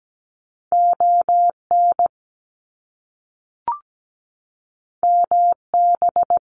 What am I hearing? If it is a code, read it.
Morse code, heard as a keyed tone, about 17 wpm, 700 Hz. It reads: ON MB
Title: Canadian Provinces and Territories 17wpm